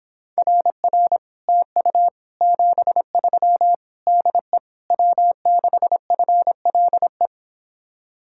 Morse code, heard as a keyed tone, about 26 wpm, 700 Hz. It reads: RR TU 73 DE W6FLE